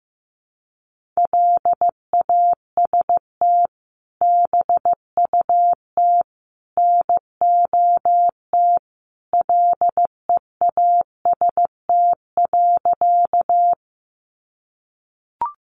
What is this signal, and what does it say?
Morse code, heard as a keyed tone, about 15 wpm, 700 Hz. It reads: LAST BUT NOT LEAST.
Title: Sentences from Top 700 Words 15wpm